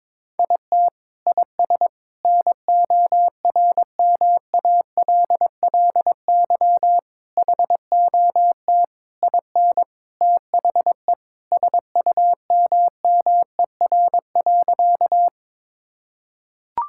Morse code, heard as a keyed tone, about 22 wpm, 700 Hz. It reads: IT IS NORMALLY HOT IN THE SUMMER.
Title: Sentences from Top 1000 Words - Review Part 6 - 22wpm